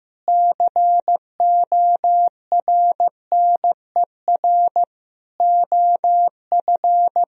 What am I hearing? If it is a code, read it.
Morse code, heard as a keyed tone, about 15 wpm, 700 Hz. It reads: CORNER OF
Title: Speed-Racing - Sets of 5 Words 15wpm